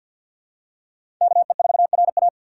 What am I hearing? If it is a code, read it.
Morse code, heard as a keyed tone, about 52 wpm, 700 Hz. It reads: KE4RA
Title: USA Call Signs - 5 Characters - 3x Character Spacing - 50wpm